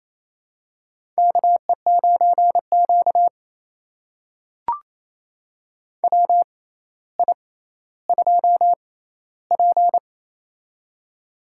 Morse code, heard as a keyed tone, about 28 wpm, spaced as for 14 wpm, 700 Hz. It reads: KE9Q WI2P
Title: USA Call Signs - 4 Characters - 6x Character Spacing - 28wpm